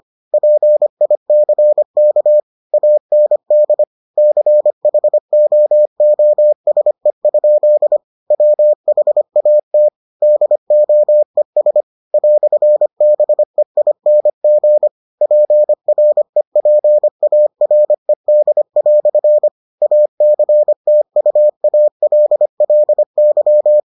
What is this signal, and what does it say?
Morse code, heard as a keyed tone, about 25 wpm, 600 Hz. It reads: PICK AND CHOOSE? WHAT DOES "BEING PREPARED" ACTUALLY